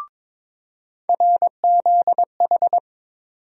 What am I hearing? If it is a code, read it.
Morse code, heard as a keyed tone, about 22 wpm, 700 Hz. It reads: RZH
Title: USA Call Sign Postfixes - 3 Letters 22wpm